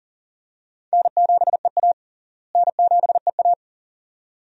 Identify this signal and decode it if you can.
Morse code, heard as a keyed tone, about 40 wpm, 700 Hz. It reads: N7EA N7EA
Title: USA Call Signs - 4 Characters - Repeated 2x - 40wpm